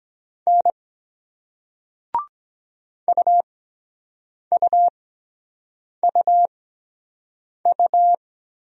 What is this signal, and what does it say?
Morse code, heard as a keyed tone, about 24 wpm, spaced as for 7 wpm, 700 Hz. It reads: N UUUU